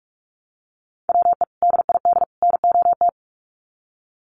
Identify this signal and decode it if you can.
Morse code, heard as a keyed tone, about 44 wpm, 700 Hz. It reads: WE DID NOT
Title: Sets of 3 Words 45wpm